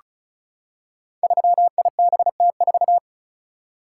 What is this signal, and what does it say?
Morse code, heard as a keyed tone, about 36 wpm, 700 Hz. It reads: 3IBT4
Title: Code Groups - 5 Letter:Numbers - 35wpm